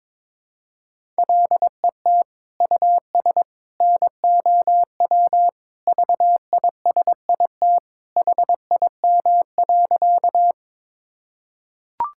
Morse code, heard as a keyed tone, about 22 wpm, 700 Hz. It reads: LET US NOW VISIT HIM.
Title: Sentences from Top 1000 Words - Review Part 6 - 22wpm